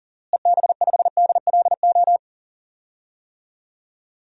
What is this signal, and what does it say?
Morse code, heard as a keyed tone, about 40 wpm, 700 Hz. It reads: EB5DLO
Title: DXCC-100 Call Signs - 6 Characters - Repeated 2x - 40wpm